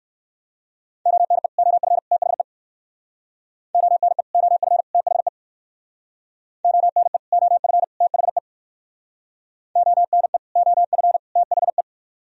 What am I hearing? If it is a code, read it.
Morse code, heard as a keyed tone, about 56 wpm, 700 Hz. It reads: ONE OF THE ONE OF THE ONE OF THE ONE OF THE